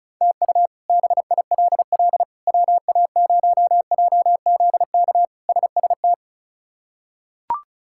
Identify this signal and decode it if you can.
Morse code, heard as a keyed tone, about 36 wpm, 700 Hz. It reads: TU BILL WA0JZK SST
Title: K1USN SST QSO Practice - 35wpm